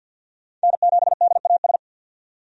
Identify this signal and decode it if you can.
Morse code, heard as a keyed tone, about 51 wpm, 700 Hz. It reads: N7DAS
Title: USA Call Signs - 5 Characters - 50wpm